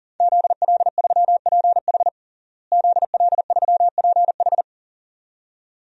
Morse code, heard as a keyed tone, about 40 wpm, 700 Hz. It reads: ZL3PH ZL3PH
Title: DXCC-100 Call Signs - 5 Characters - Repeated 2x - 40wpm